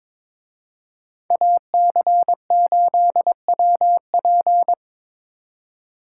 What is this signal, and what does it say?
Morse code, heard as a keyed tone, about 24 wpm, 700 Hz. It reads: AC8WP
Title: Speed Racing - USA 2x2 Call Signs 22wpm